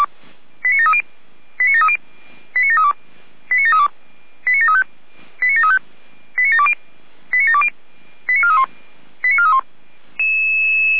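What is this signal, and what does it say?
Five-tone selective call (ZVEI), heard as quick runs of five tones, tones of about 70 ms. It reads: 89730 89730 89732 89732 89735 89735 89720 89720 89431 89431
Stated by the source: ZVEI five-tone call received by SDR.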